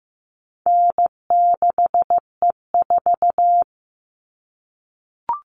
Morse code, heard as a keyed tone, about 15 wpm, 700 Hz. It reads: N6E4
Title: All Cut Numbers - 4-Digit 15wpm